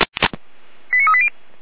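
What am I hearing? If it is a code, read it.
Five-tone selective call (ZVEI), heard as quick runs of five tones, tones of about 70 ms. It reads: 89380
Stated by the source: ZVEI five-tone call received by SDR, from audio.f85360.feuerwehr-orlamuende.org (Feuerwehr Orlamünde).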